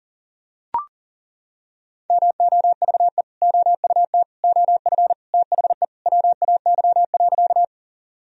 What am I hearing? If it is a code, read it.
Morse code, heard as a keyed tone, about 40 wpm, 700 Hz. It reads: MOVE OUT OF THE WAY.
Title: Sentences from Top 1000 Words - Review Part 1 - 40wpm